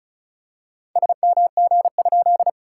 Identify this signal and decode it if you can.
Morse code, heard as a keyed tone, about 36 wpm, 690 Hz. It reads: SMG?
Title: SOTA QSO Practice 35wpm